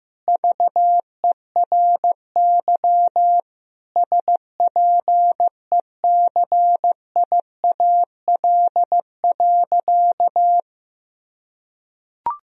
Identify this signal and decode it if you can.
Morse code, heard as a keyed tone, about 15 wpm, 700 Hz. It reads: VERY SPECIAL.
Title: Sentences from Top 1000 Words - Review Part 3 - 15wpm